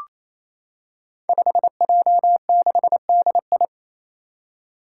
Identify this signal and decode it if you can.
Morse code, heard as a keyed tone, about 28 wpm, 700 Hz. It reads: 5J6DI
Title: Five Challenging Letter-Number Combinations 28wpm